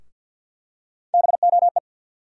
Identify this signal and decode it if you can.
Morse code, heard as a keyed tone, about 52 wpm, 700 Hz. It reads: BOE
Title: USA Call Sign Postfixes - 3 Letters 50wpm